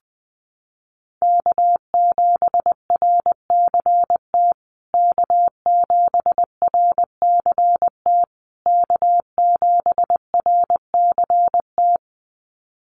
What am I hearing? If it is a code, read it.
Morse code, heard as a keyed tone, about 20 wpm, 700 Hz. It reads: K7RCT K7RCT K7RCT
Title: USA Call Signs Repeated 3x 20wpm